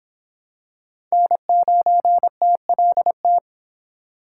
Morse code, heard as a keyed tone, about 24 wpm, 700 Hz. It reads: N9TLT